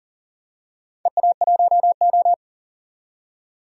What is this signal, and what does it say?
Morse code, heard as a keyed tone, about 40 wpm, 700 Hz. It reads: EA1O
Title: DXCC-100 Call Signs - 4 Characters - 40wpm